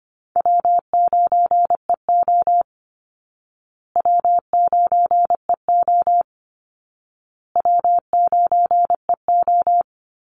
Morse code, heard as a keyed tone, about 25 wpm, 700 Hz. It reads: W9EO W9EO W9EO
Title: Call Signs Repeated 2x - 25wpm